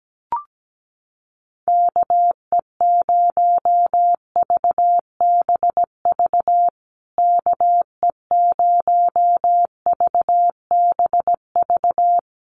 Morse code, heard as a keyed tone, about 17 wpm, 700 Hz. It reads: KE0VBV KE0VBV
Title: USA Call Signs Repeated 3x 17wpm